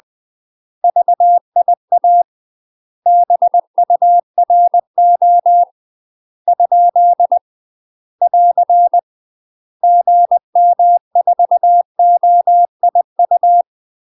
Morse code, heard as a keyed tone, about 20 wpm, 700 Hz. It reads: VIA BURO ? <AR> GM4OIU
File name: UK MASTER QSOs TRAINING / 20WPM / UK MASTER QSOs 0048